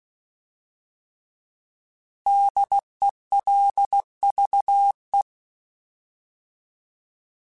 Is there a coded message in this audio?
Yes: Morse code from 2.3 to 5.2 s, heard as a keyed tone, about 16 wpm, 800 Hz. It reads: DELVE